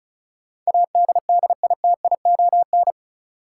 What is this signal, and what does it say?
Morse code, heard as a keyed tone, about 34 wpm, 700 Hz. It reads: ADDITION
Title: Sentences from Top 1500 Words - New Word Drill Alt - 35wpm